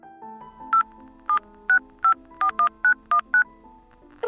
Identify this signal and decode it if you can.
DTMF, heard as telephone touch tones, tones of about 100 ms. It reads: 7 411